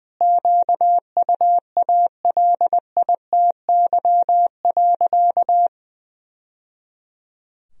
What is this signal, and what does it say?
Morse code, heard as a keyed tone, about 19 wpm, 700 Hz. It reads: QUALITY.
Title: Sentences from Top 1000 Words - Review Part 4 - 20wpm